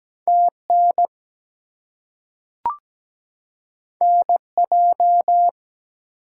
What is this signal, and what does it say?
Morse code, heard as a keyed tone, about 17 wpm, 700 Hz. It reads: TN NJ
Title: US States and Canadian Provinces 17wpm